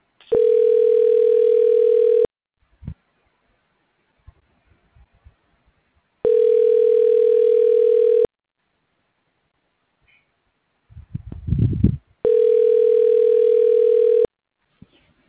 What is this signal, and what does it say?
Ringback tone (North America), 440 + 480 Hz.